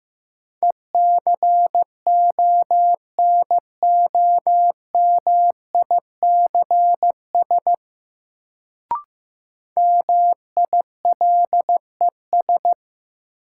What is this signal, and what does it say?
Morse code, heard as a keyed tone, about 15 wpm, 700 Hz. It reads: ECONOMICS MILES
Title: Sentences from Top 1000 Words - Review Part 3 - New Word Drill - 15wpm